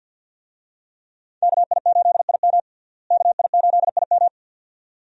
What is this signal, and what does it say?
Morse code, heard as a keyed tone, about 47 wpm, 700 Hz. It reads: KI8IM KI8IM
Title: USA Call Signs - 5 Characters - Repeated 2x - 50wpm